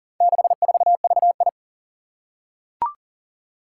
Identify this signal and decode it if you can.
Morse code, heard as a keyed tone, about 40 wpm, 700 Hz. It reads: 64VI
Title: Four Challenging Letter-Number Combinations 40wpm